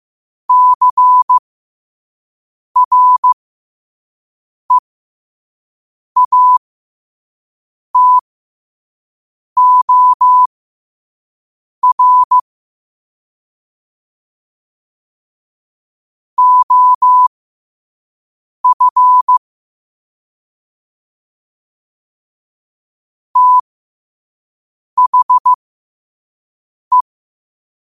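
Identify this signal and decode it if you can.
Morse code, heard as a keyed tone, about 15 wpm, spaced as for 5 wpm, 1000 Hz. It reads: CREATOR OF THE
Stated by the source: Morse code sent at 5 wpm for learning.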